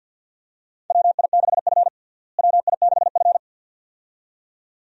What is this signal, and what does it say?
Morse code, heard as a keyed tone, about 51 wpm, 700 Hz. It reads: WI6F WI6F
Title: USA Call Signs - 4 Characters - Repeated 2x - 50wpm